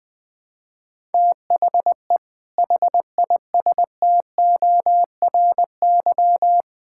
Morse code, heard as a keyed tone, about 21 wpm, 700 Hz. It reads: THE HISTORY
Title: Build A Phrase - 3-Word Phrases - 20wpm